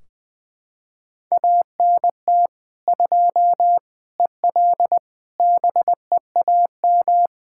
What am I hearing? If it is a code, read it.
Morse code, heard as a keyed tone, about 21 wpm, 700 Hz. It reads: ANT 2 EL BEAM
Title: Practice QSO - Exchange 2 - Info - 20wpm